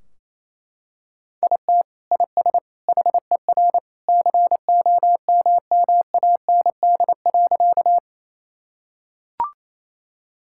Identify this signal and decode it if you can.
Morse code, heard as a keyed tone, about 28 wpm, 700 Hz. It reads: IT IS HER COMMAND.
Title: Sentences from Top 1600 Words 28wpm